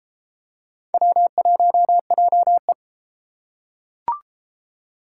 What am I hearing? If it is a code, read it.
Morse code, heard as a keyed tone, about 31 wpm, 700 Hz. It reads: W1JE